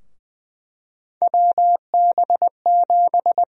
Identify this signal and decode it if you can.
Morse code, heard as a keyed tone, about 20 wpm, 700 Hz. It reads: WB7
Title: USA Call Sign Prefixes - Repeated 2x - 20wpm